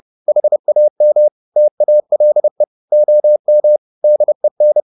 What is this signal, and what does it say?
Morse code, heard as a keyed tone, about 30 wpm, 600 Hz. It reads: HAM TALE OM DEN
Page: Ferie 30 wpm